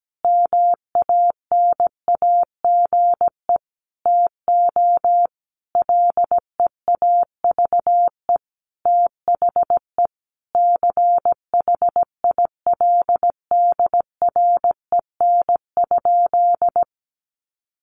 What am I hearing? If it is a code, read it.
Morse code, heard as a keyed tone, about 17 wpm, 700 Hz. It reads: MANAGE TO LEAVE THE CHILDREN?
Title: Sentences from Top 1400 Words 17wpm